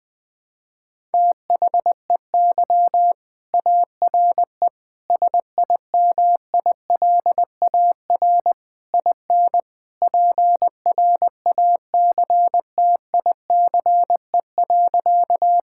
Morse code, heard as a keyed tone, about 20 wpm, 700 Hz. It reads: THEY ARE SIMILAR IN PRACTICE.
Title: Sentences from Top 1000 Words - Review Part 5 - 20wpm